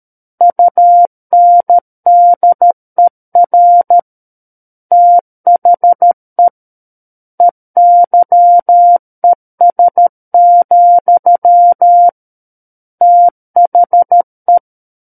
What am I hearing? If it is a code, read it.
Morse code, heard as a keyed tone, about 13 wpm, 700 Hz. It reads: UNDER THE EYES, THE
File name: Book One / 04-Chapter 4 - 13 WPM